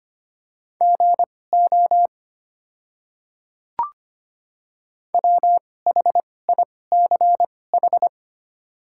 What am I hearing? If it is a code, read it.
Morse code, heard as a keyed tone, about 25 wpm, spaced as for 18 wpm, 690 Hz. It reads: GO WHICH
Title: 2x Character Spacing - Top 100 Words - 25wpm